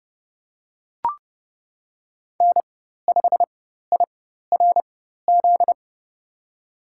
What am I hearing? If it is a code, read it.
Morse code, heard as a keyed tone, about 30 wpm, spaced as for 14 wpm, 700 Hz. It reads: N5IRZ
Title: USA Call Signs - 5 Characters - 4x Character Spacing - 30wpm